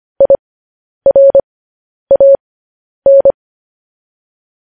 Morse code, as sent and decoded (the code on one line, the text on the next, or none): .. .-. .- -.
IRAN